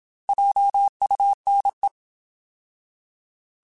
Morse code, heard as a keyed tone, about 27 wpm, 800 Hz. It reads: JUNE
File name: Alphabetical / Alphabetical mp3s / Js @25